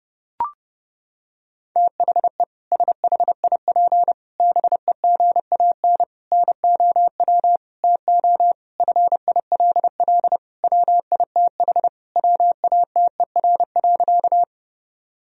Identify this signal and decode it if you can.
Morse code, heard as a keyed tone, about 32 wpm, 700 Hz. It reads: THE SHIP BEGAN NOW TO FILL WITH WATER.
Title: Sentences from Top 1000 Words - Review Part 3 - 30wpm